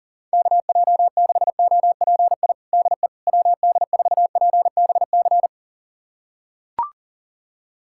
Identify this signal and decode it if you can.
Morse code, heard as a keyed tone, about 37 wpm, 700 Hz. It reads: KJ6OPI DE WD4PBC
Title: USA Callsign DE Callsign - 6 Characters - 40wpm